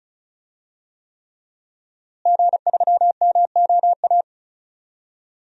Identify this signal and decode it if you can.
Morse code, heard as a keyed tone, about 34 wpm, 700 Hz. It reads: G3MOA
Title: DXCC-100 Call Signs - 5 Characters - Repeated 3x - 35wpm